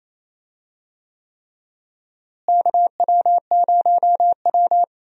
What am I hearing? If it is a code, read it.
Morse code, heard as a keyed tone, about 28 wpm, 700 Hz. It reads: KW0W
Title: USA Call Signs - 4 Characters - Repeated 3x - 28wpm